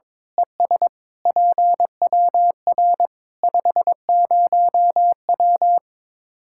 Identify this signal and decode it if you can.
Morse code, heard as a keyed tone, about 22 wpm, 700 Hz. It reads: ES PWR 50W
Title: Practice QSO - Exchange 2 - Info - 22wpm